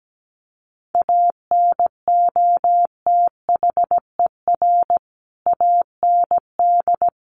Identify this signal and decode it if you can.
Morse code, heard as a keyed tone, about 17 wpm, 700 Hz. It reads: ANOTHER AND
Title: Sentences from Top 600 Words 17wpm